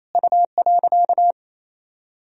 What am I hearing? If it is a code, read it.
Morse code, heard as a keyed tone, about 28 wpm, 700 Hz. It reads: U.
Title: Sentences from Top 200 Words and Abbreviations 28wpm